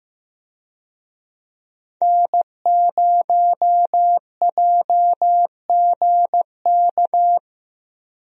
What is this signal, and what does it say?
Morse code, heard as a keyed tone, about 15 wpm, 700 Hz. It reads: N0JGK
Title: USA Call Signs - 5 Characters - Repeated 2x - 15wpm